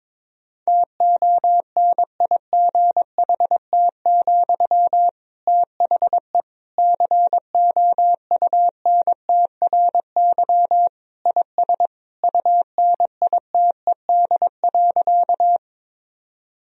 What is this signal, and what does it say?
Morse code, heard as a keyed tone, about 21 wpm, 700 Hz. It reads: TONIGHT, THE COUNTRY IS UNITED.
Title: Sentences from Top 1000 Words - Review Part 4 - 22wpm